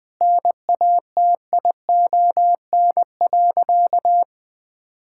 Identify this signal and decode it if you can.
Morse code, heard as a keyed tone, about 20 wpm, 700 Hz. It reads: NATION.
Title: Sentences with 8-Word Phrases - 20wpm